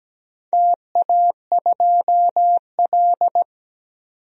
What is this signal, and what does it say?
Morse code, heard as a keyed tone, about 17 wpm, 700 Hz. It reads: TA2L
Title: DXCC-100 Call Signs - 4 Characters - 17wpm